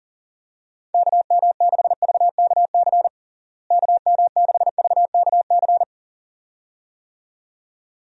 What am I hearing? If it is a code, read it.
Morse code, heard as a keyed tone, about 40 wpm, 700 Hz. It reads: KM6VKC KM6VKC
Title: USA Call Signs - 6 Characters - Repeated 2x - 40wpm